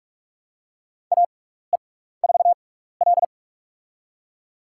Morse code, heard as a keyed tone, about 46 wpm, spaced as for 16 wpm, 700 Hz. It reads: AE4L
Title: USA Call Signs - 6x Character Spacing - 45wpm